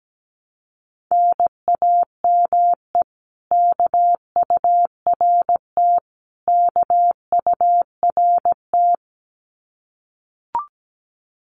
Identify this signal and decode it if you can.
Morse code, heard as a keyed tone, about 17 wpm, 700 Hz. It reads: NAME KURT KURT
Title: Common QSO Phrases Encore 17wpm